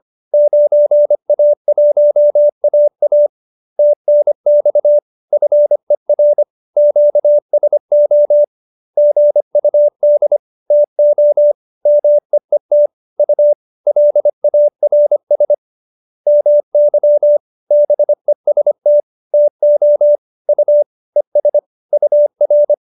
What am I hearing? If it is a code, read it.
Morse code, heard as a keyed tone, about 25 wpm, 590 Hz. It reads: 9A1AA TNX FER QSO GUD TO MEET U LARS MY BEST TO U ES UR